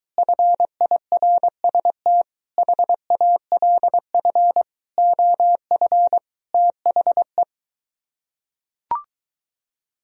Morse code, heard as a keyed tone, about 23 wpm, 700 Hz. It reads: FIRST HALF OF THE